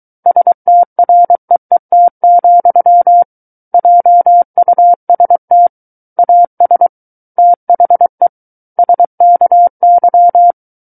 Morse code, heard as a keyed tone, about 23 wpm, 700 Hz. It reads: STREET, JUST AS THE SKY
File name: Book One / 14-Chapter 14 - 23 WPM